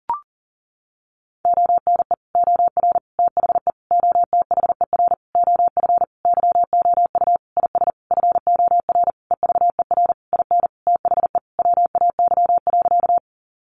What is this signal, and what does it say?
Morse code, heard as a keyed tone, about 40 wpm, 700 Hz. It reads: ONE OR THE OTHER OF YOU IS FOR EVER IN THE WAY.
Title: Sentences from Top 400 Words 40wpm